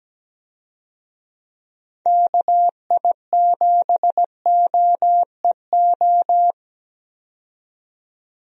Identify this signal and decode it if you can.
Morse code, heard as a keyed tone, about 17 wpm, 700 Hz. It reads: KI7OEO
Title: USA Call Signs - 6 Characters - 17wpm